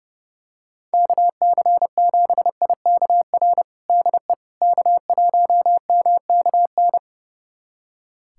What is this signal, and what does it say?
Morse code, heard as a keyed tone, about 32 wpm, 700 Hz. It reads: KC7IKR DE K1MKN
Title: USA Callsign DE Callsign 30wpm